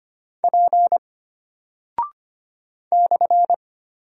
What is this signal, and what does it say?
Morse code, heard as a keyed tone, about 25 wpm, 700 Hz. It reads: P /
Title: Single Character (Letter:Number:Punctuation:Prosign) Alt 25wpm